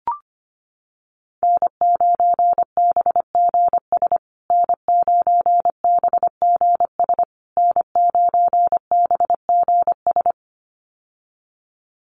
Morse code, heard as a keyed tone, about 25 wpm, 700 Hz. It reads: N9BGS N9BGS N9BGS
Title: USA Call Signs Repeated 3x 25wpm